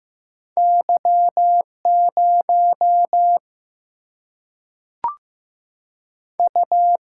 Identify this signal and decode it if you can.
Morse code, heard as a keyed tone, about 15 wpm, 700 Hz. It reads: Y 0 U
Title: Single-Letter Single-Number 15wpm